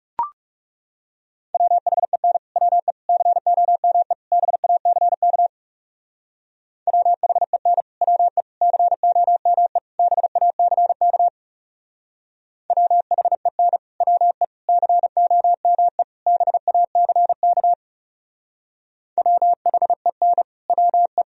Morse code, heard as a keyed tone, about 38 wpm, 700 Hz. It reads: WHEN WE COME BACK WHEN WE COME BACK WHEN WE COME BACK WHEN WE